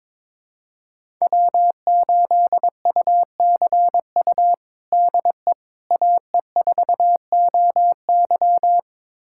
Morse code, heard as a keyed tone, about 22 wpm, 700 Hz. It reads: W8UCU DE AE4OY
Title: USA Callsign DE Callsign 22wpm